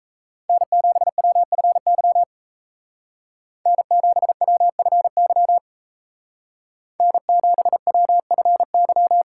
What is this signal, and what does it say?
Morse code, heard as a keyed tone, about 36 wpm, 700 Hz. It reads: N7WFY N7WFY N7WFY